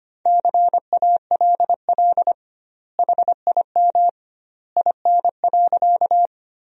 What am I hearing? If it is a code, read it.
Morse code, heard as a keyed tone, about 25 wpm, 700 Hz. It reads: CALL HIM IN.
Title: 2x Word-Spacing - Sentences from Top 100 Words - 25wpm